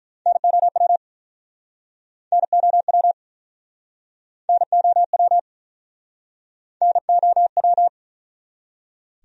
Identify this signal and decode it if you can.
Morse code, heard as a keyed tone, about 49 wpm, 700 Hz. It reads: NOW NOW NOW NOW